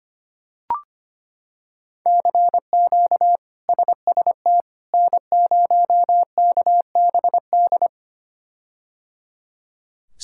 Morse code, heard as a keyed tone, about 25 wpm, 700 Hz. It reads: CQ SST N0KBD
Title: K1USN SST QSO Practice - 25wpm